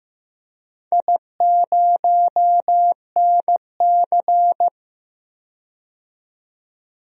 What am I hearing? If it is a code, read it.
Morse code, heard as a keyed tone, about 15 wpm, 700 Hz. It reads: I0NC
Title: DXCC-100 Call Signs - 4 Characters - 15wpm